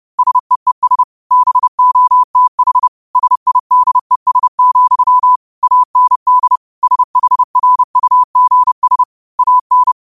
Morse code, heard as a keyed tone, about 30 wpm, 1000 Hz. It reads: SEES BOTH SIDES, AND SHRUGS AN
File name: qotdfiles / 30wpm